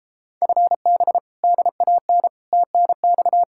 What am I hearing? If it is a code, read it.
Morse code, heard as a keyed tone, about 33 wpm, 700 Hz. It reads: FB DAN TNX